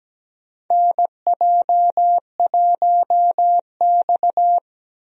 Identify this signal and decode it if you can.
Morse code, heard as a keyed tone, about 17 wpm, 700 Hz. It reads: NJ1X
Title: USA Call Signs - 4 Characters - Repeated 3x - 17wpm